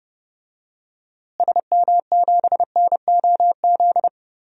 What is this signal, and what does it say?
Morse code, heard as a keyed tone, about 31 wpm, 700 Hz. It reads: SM7NOZ
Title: DXCC-100 Call Signs - 6 Characters - Repeated 3x - 30wpm